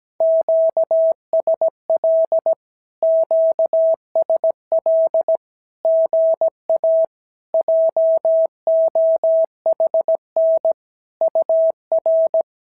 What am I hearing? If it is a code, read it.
Morse code, heard as a keyed tone, about 17 wpm, 650 Hz. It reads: QSL QSL GA JOHN UR
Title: SOTA QSO Practice 17wpm